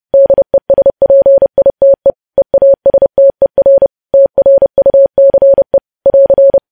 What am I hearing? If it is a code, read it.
Morse code, heard as a keyed tone, about 30 wpm, 550 Hz. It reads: DESPITE EASTER TRUCE <AR>
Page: news headlines in morse code at 30 wpm